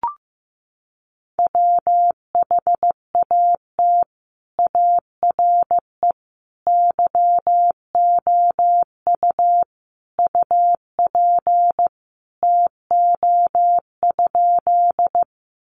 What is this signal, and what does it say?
Morse code, heard as a keyed tone, about 15 wpm, 700 Hz. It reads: WHAT ARE YOU UP TO?
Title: Sentences from Top 100 Words 15wpm